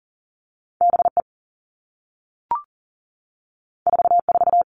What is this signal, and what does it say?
Morse code, heard as a keyed tone, about 40 wpm, 700 Hz. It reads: BE 44
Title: Two Challenging Letter-Number Combinations 40wpm